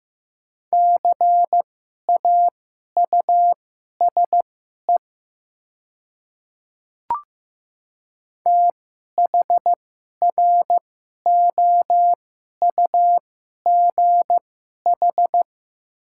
Morse code, heard as a keyed tone, about 15 wpm, spaced as for 11 wpm, 700 Hz. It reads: CAUSE THROUGH
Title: 2x Character Spacing - Top 300 Words - 15wpm